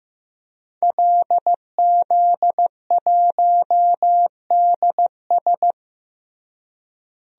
Morse code, heard as a keyed tone, about 15 wpm, 700 Hz. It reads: LZ1DS
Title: DXCC-100 Call Signs - 5 Characters - Repeated 2x - 15wpm